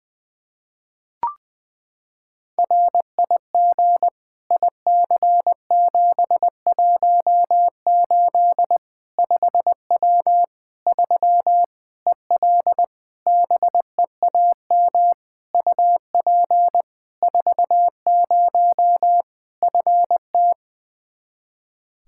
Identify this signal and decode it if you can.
Morse code, heard as a keyed tone, about 20 wpm, 700 Hz. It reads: RIG IC718 5W 3 EL BEAM UP 40 FT
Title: K1USN SST QSO Practice 20wpm